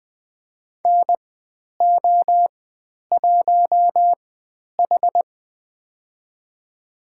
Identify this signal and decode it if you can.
Morse code, heard as a keyed tone, about 20 wpm, spaced as for 10 wpm, 700 Hz. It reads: NO1H
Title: 095 Lesson 16g - Letter c - Call Signs 20wpm